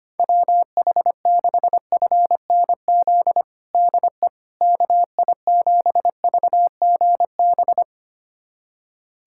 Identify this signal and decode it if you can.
Morse code, heard as a keyed tone, about 25 wpm, 700 Hz. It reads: WH6FNZ DE KI7VGB
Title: USA Callsign DE Callsign - 6 Characters - 25wpm